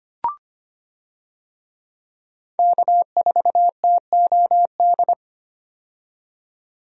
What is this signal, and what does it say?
Morse code, heard as a keyed tone, about 25 wpm, 700 Hz. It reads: K4TOD
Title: USA Call Signs - 5 Characters - 25wpm